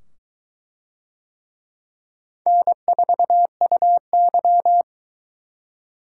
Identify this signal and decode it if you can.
Morse code, heard as a keyed tone, about 20 wpm, 700 Hz. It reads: N4UY